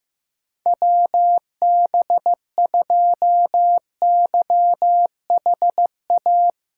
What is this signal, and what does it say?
Morse code, heard as a keyed tone, about 15 wpm, 700 Hz. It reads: WB2YHA
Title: USA Callsign DE Callsign - 6 Characters - 15wpm